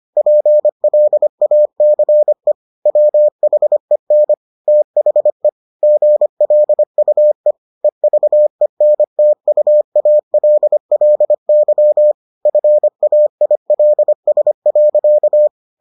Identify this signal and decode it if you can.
Morse code, heard as a keyed tone, about 25 wpm, 600 Hz. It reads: PLACE WHEN THE GLUE EVENTUALLY FAILS.